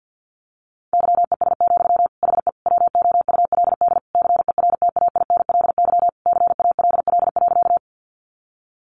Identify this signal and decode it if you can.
Morse code, heard as a keyed tone, about 49 wpm, 700 Hz. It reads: YES, HE WOULD CERTAINLY CALL.
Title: Sentences from Top 900 Words 50wpm